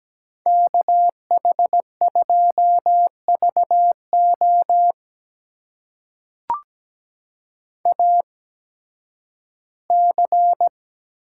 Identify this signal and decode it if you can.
Morse code, heard as a keyed tone, about 17 wpm, spaced as for 11 wpm, 700 Hz. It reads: KH2VO AC
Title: USA Call Signs - 5 Characters - 8x Character Spacing - 17wpm